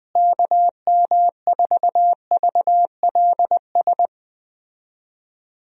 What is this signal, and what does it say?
Morse code, heard as a keyed tone, about 20 wpm, 700 Hz. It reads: KM4VLS
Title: USA Call Signs - 6 Characters - Repeated 3x - 20wpm